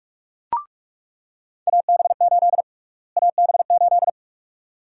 Morse code, heard as a keyed tone, about 41 wpm, 700 Hz. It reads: AB8 AB8
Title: USA Call Sign Prefixes - Repeated 2x - 45wpm